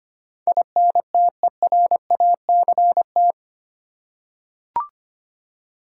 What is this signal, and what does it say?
Morse code, heard as a keyed tone, about 25 wpm, 700 Hz. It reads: INTERACT
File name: 8 Letter Words - First 3 Letters Given - 25wpm